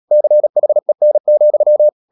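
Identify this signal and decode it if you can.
Morse code, heard as a keyed tone, about 37 wpm, 600 Hz. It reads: CHEN,
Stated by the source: Morse code practice at 35–39 wpm sending English text of Ken Follett's novel Never, from CW med Gnister.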